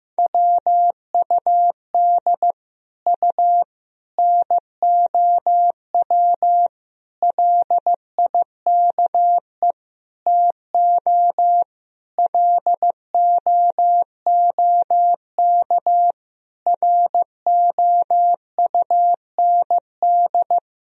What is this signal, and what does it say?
Morse code, heard as a keyed tone, about 15 wpm, 700 Hz. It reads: WUD U NOW LIKE TO LOOK ROUND
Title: Sentences from Top 200 Words and Abbreviations 15wpm